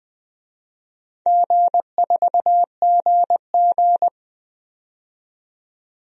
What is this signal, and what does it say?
Morse code, heard as a keyed tone, about 20 wpm, 700 Hz. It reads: G4GG
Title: DXCC-100 Call Signs - 4 Characters - 20wpm